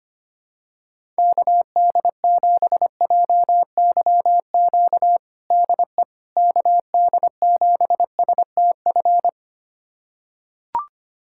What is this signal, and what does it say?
Morse code, heard as a keyed tone, about 25 wpm, 700 Hz. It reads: KD7JYQ DE KD7STF
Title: USA Callsign DE Callsign 25wpm